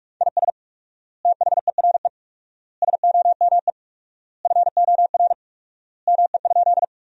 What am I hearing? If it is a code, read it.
Morse code, heard as a keyed tone, about 46 wpm, 700 Hz. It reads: IS THERE SOME FOR ME?
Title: 4x Word-Spacing - Sentences from Top 200 Words - 45wpm